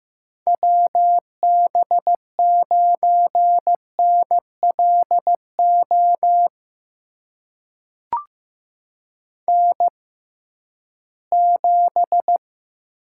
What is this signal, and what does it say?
Morse code, heard as a keyed tone, about 15 wpm, spaced as for 11 wpm, 700 Hz. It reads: WB9NLO N7
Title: USA Call Signs - 6x Character Spacing - 15wpm